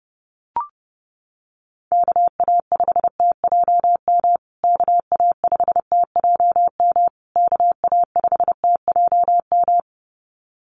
Morse code, heard as a keyed tone, about 30 wpm, 700 Hz. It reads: KA5TJM KA5TJM KA5TJM
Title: USA Call Signs Repeated 3x 30wpm